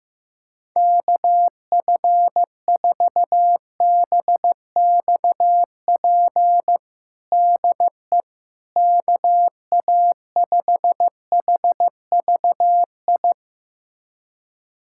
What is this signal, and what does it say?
Morse code, heard as a keyed tone, about 15 wpm, 700 Hz. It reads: KF4BXP DE KA5HVI
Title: USA Callsign DE Callsign - 6 Characters - 15wpm